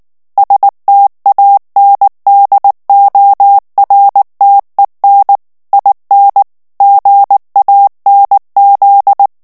Morse code, heard as a keyed tone, about 19 wpm, 800 Hz. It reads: STANDORTEN IN GANZ
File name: morse mp3 / ALLEMAND texte clair 2 vitesse 18 mots minute